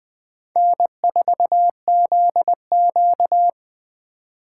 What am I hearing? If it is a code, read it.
Morse code, heard as a keyed tone, about 25 wpm, 700 Hz. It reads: N4ZQ
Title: Speed Racing - USA 1x2 Call Signs 20wpm